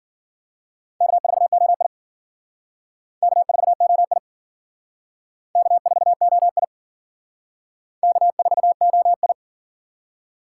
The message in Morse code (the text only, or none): K4OI K4OI K4OI K4OI